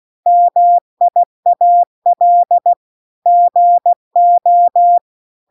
Morse code, heard as a keyed tone, about 16 wpm, 700 Hz. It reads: MIAL GO